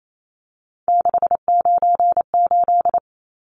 Morse code, heard as a keyed tone, about 28 wpm, 700 Hz. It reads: 698
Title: Numbers - 3-Digit 28wpm